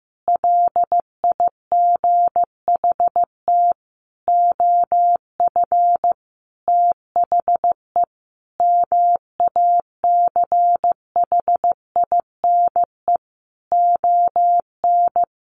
Morse code, heard as a keyed tone, about 15 wpm, 700 Hz. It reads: LIGHT OF THE MACHINE ON
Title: A Sound Of Thunder 15wpm